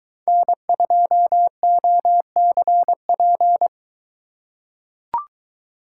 Morse code, heard as a keyed tone, about 22 wpm, 700 Hz. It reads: N2OCP